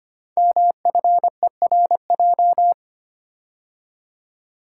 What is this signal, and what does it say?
Morse code, heard as a keyed tone, about 25 wpm, 700 Hz. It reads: MFERJ
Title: Code Groups - 5 Letters - 25wpm